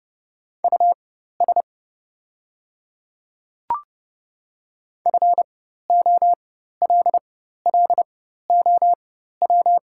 Morse code, heard as a keyed tone, about 31 wpm, spaced as for 14 wpm, 700 Hz. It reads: US FOLLOW
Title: 4x Character Spacing - Top 200 Words - 30wpm